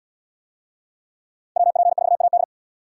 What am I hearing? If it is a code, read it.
Morse code, heard as a keyed tone, about 77 wpm, 700 Hz. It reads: WP4AD